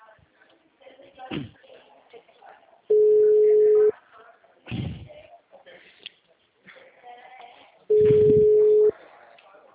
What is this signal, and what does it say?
Ringback tone (Europe, 425 Hz), with speech.